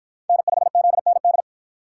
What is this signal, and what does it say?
Morse code, heard as a keyed tone, about 50 wpm, 700 Hz. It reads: N5ZND